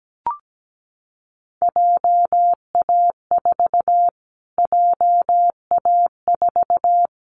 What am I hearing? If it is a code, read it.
Morse code, heard as a keyed tone, about 17 wpm, 700 Hz. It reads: JA4 JA4
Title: Callsign Prefixes for Top 25 DXCC - Repeated 3x - 17wpm